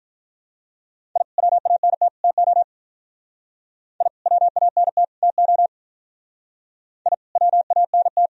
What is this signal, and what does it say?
Morse code, heard as a keyed tone, about 45 wpm, 700 Hz. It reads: I WANT TO I WANT TO I WANT